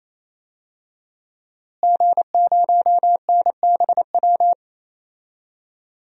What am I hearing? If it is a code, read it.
Morse code, heard as a keyed tone, about 28 wpm, 700 Hz. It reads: G0NBW
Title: DXCC-100 Call Signs - 5 Characters - Repeated 3x - 28wpm